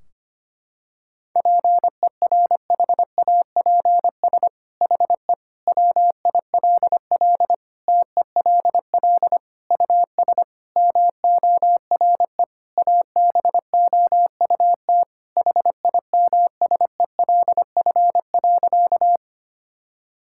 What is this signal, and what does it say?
Morse code, heard as a keyed tone, about 25 wpm, 700 Hz. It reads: PERHAPS HE WILL TELL US MORE ABOUT HIMSELF.
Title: Sentences from Top 1600 Words 25wpm